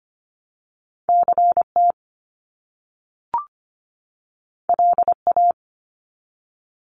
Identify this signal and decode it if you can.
Morse code, heard as a keyed tone, about 25 wpm, 700 Hz. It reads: CT LA
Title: US States 25wpm